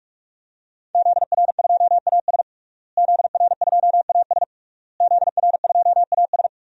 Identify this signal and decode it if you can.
Morse code, heard as a keyed tone, about 46 wpm, 700 Hz. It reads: ZR2AS ZR2AS ZR2AS
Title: DXCC-100 Call Signs - 5 Characters - Repeated 3x - 45wpm